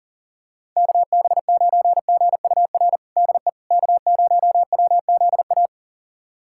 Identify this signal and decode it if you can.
Morse code, heard as a keyed tone, about 37 wpm, 700 Hz. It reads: KB9GUR DE K0WZA
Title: USA Callsign DE Callsign 40wpm